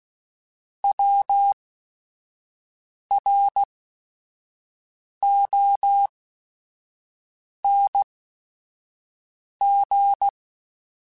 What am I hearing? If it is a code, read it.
Morse code, heard as a keyed tone, about 16 wpm, spaced as for 5 wpm, 800 Hz. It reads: WRONG
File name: Short Runs / Short Runs-mp3s / Words @5